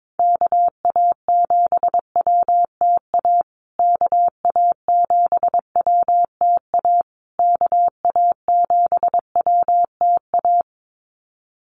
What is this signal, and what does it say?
Morse code, heard as a keyed tone, about 22 wpm, 700 Hz. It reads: KA7WTA KA7WTA KA7WTA
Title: USA Call Signs Repeated 3x 22wpm